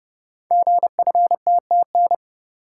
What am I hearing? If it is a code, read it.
Morse code, heard as a keyed tone, about 29 wpm, 700 Hz. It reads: GFTTN
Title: Code Groups - 5 Letters - 30wpm